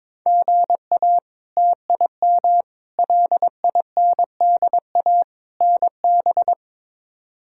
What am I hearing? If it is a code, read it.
Morse code, heard as a keyed tone, about 22 wpm, 700 Hz. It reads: GA TIM LINDA NB
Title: K1USN SST QSO Practice - 22wpm